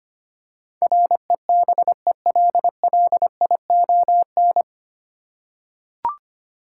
Morse code, heard as a keyed tone, about 25 wpm, 700 Hz. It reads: REBELLION
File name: Nine Letter Words - 25wpm